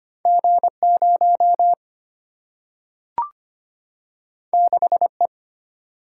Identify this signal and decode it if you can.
Morse code, heard as a keyed tone, about 25 wpm, 700 Hz. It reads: G0 6E
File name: Two Letter-Number Combinations - 25wpm